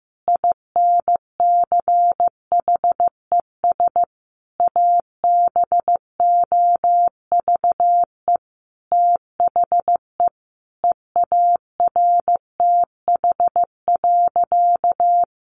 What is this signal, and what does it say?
Morse code, heard as a keyed tone, about 15 wpm, 700 Hz. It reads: INCHES ABOVE THE EARTH.
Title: A Sound Of Thunder 15wpm